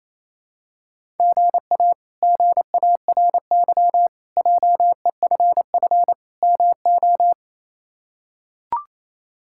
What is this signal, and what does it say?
Morse code, heard as a keyed tone, about 28 wpm, 700 Hz. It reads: GA GARY JEFF MO
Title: K1USN SST QSO Practice - 28wpm